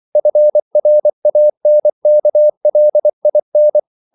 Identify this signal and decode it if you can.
Morse code, heard as a keyed tone, about 24 wpm, 600 Hz. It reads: FRANKLIN
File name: CW / Practice / EnglishQuotations0424